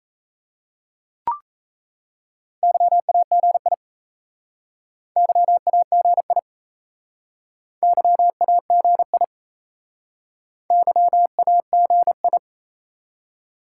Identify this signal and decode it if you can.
Morse code, heard as a keyed tone, about 36 wpm, 700 Hz. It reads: YAGI YAGI YAGI YAGI